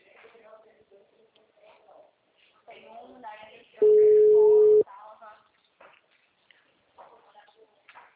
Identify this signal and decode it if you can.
Ringback tone (Europe, 425 Hz), with speech.